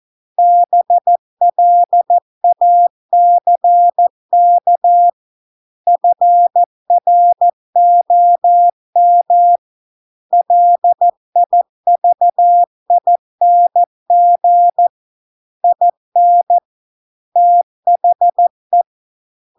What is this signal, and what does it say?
Morse code, heard as a keyed tone, about 14 wpm, 700 Hz. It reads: BLACK FROM LIVING IN THE